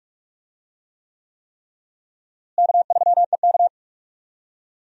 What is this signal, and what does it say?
Morse code, heard as a keyed tone, about 46 wpm, 700 Hz. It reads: K3EK
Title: USA Call Signs - 4 Characters - Repeated 3x - 45wpm